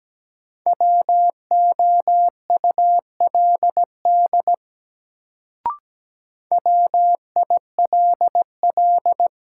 Morse code, heard as a keyed tone, about 17 wpm, 700 Hz. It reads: WOULD WILL
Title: Sentences from Top 100 Words - New Word Drill - 17wpm